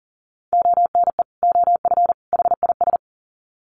Morse code, heard as a keyed tone, about 40 wpm, 700 Hz. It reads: ONE OF HIS
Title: Sets of 3 Words 40wpm